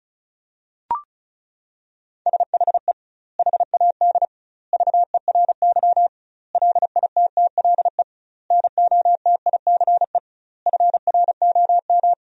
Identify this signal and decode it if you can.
Morse code, heard as a keyed tone, about 36 wpm, 700 Hz. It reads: SHE HAD VERY LITTLE NOTICE FROM
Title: 2x Word-Spacing - Sentences from Top 400 Words - 35wpm